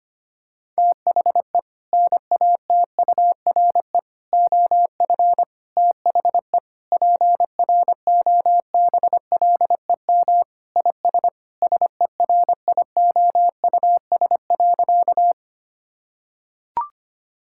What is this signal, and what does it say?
Morse code, heard as a keyed tone, about 25 wpm, 700 Hz. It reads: THE NATURE OF THE PROBLEM IS SERIOUS.
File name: practice / 25wpm / Sentences with 3-Word Phrases - 25wpm